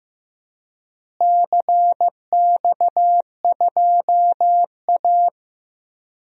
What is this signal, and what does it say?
Morse code, heard as a keyed tone, about 15 wpm, 700 Hz. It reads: CX2A
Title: DXCC-100 Call Signs - 4 Characters - Repeated 3x - 15wpm